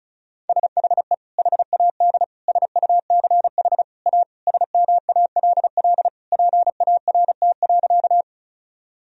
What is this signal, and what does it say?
Morse code, heard as a keyed tone, about 36 wpm, 700 Hz. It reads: SHE HAD SUCH A SMALL PART.
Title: Sentences from Top 1000 Words - Review Part 1 - 35wpm